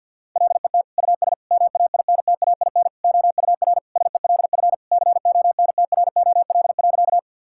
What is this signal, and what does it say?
Morse code, heard as a keyed tone, about 50 wpm, 700 Hz. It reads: LET US MAINTAIN OUR SELF CONTROL.
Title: Sentences from Top 1000 Words - Review Part 5 - 50wpm